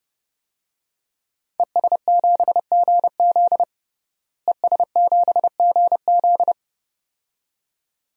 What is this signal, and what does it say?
Morse code, heard as a keyed tone, about 30 wpm, 700 Hz. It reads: ES7GZ ES7GZ
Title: DXCC-100 Call Signs - 5 Characters - Repeated 2x - 30wpm